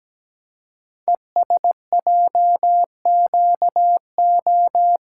Morse code, heard as a keyed tone, about 17 wpm, 700 Hz. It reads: ESJQO
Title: Code Groups - 5 Letters - 17wpm